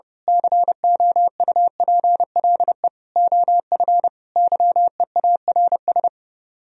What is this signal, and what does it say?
Morse code, heard as a keyed tone, about 28 wpm, 700 Hz. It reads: COUPLE OF YEARS
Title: Speed-Racing - Sets of 5 Words 30wpm